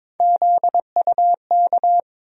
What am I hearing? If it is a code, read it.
Morse code, heard as a keyed tone, about 21 wpm, 700 Hz. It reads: ZUK
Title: USA Call Sign Postfixes - 3 Letters 22wpm